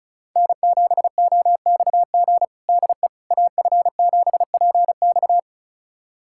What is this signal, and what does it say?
Morse code, heard as a keyed tone, about 34 wpm, 700 Hz. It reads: N7OXG DE AF7PX
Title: USA Callsign DE Callsign - 5 Characters - 35wpm